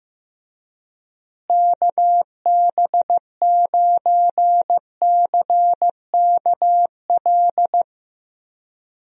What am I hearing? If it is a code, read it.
Morse code, heard as a keyed tone, about 15 wpm, 700 Hz. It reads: KB9CKL
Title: USA Call Signs - 6 Characters - Repeated 2x - 15wpm